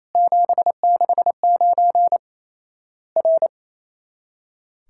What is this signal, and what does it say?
Morse code, heard as a keyed tone, about 28 wpm, 700 Hz. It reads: 769 R
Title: Worked All Europe DX Contest - QTC Exchange - 28wpm